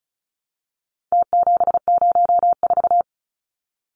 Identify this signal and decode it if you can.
Morse code, heard as a keyed tone, about 34 wpm, 700 Hz. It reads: T704
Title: Numbers - Contest Style - 4-Digit 35wpm